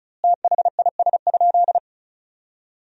Morse code, heard as a keyed tone, about 36 wpm, 700 Hz. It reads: THIS?
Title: 2x Word-Spacing - Sentences from Top 400 Words - 35wpm